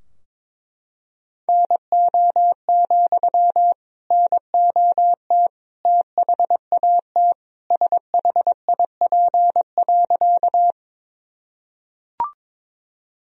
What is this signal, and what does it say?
Morse code, heard as a keyed tone, about 22 wpm, 700 Hz. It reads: NO, NOT THAT SHIP.
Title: Sentences from Top 1000 Words - Review Part 2 - 22wpm